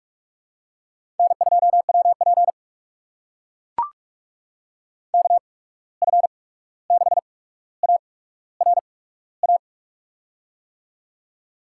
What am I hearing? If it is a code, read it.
Morse code, heard as a keyed tone, about 44 wpm, spaced as for 17 wpm, 700 Hz. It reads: N2WP KF6ARA
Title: USA Call Signs - 8x Character Spacing - 45wpm